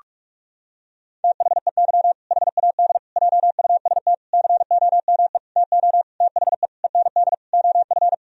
Morse code, heard as a keyed tone, about 46 wpm, 700 Hz. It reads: THEY HAD JUST COME TO THE END OF
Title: Sentences from Top 1000 Words - Review Part 2 - 45wpm